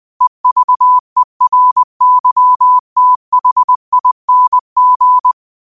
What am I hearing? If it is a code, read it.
Morse code, heard as a keyed tone, about 20 wpm, 1000 Hz. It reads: EVERYTHING